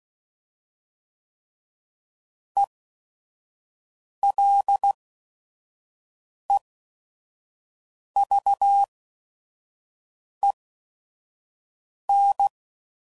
Morse code, heard as a keyed tone, about 16 wpm, spaced as for 5 wpm, 800 Hz. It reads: ELEVEN